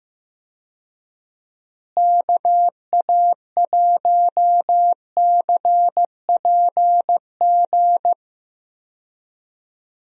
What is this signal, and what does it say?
Morse code, heard as a keyed tone, about 15 wpm, 700 Hz. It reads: KA1CPG
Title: USA Call Signs - 6 Characters - Repeated 2x - 15wpm